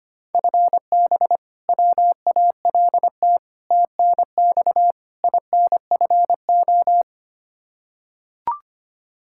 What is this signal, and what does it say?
Morse code, heard as a keyed tone, about 25 wpm, 700 Hz. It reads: FB WALT TNX INFO
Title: Practice QSO - Exchange 3 - More Info - 25wpm